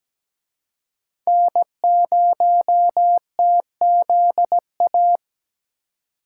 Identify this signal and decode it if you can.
Morse code, heard as a keyed tone, about 20 wpm, 700 Hz. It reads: N0TZA
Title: Speed Racing - USA 1x3 Call Signs 17wpm